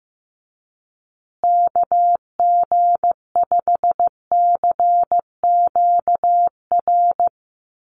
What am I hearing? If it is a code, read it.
Morse code, heard as a keyed tone, about 15 wpm, 700 Hz. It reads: KG5CQR
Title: USA Call Signs Repeated 3x 15wpm